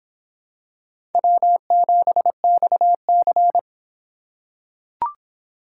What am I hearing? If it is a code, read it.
Morse code, heard as a keyed tone, about 24 wpm, 700 Hz. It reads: W7XC